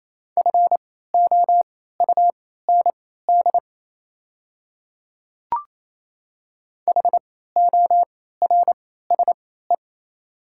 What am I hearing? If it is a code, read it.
Morse code, heard as a keyed tone, about 27 wpm, spaced as for 16 wpm, 690 Hz. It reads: FOUND HORSE
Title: 3x Character Spacing - Top 500 Words - 28wpm